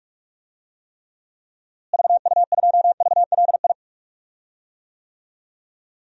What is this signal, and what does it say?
Morse code, heard as a keyed tone, about 44 wpm, 700 Hz. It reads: VU2VLI
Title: DXCC-100 Call Signs - 6 Characters - Repeated 2x - 45wpm